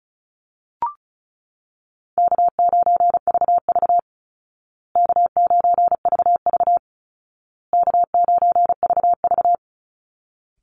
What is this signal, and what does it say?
Morse code, heard as a keyed tone, about 34 wpm, 700 Hz. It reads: K9VV K9VV K9VV
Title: Call Signs Repeated 2x - 35wpm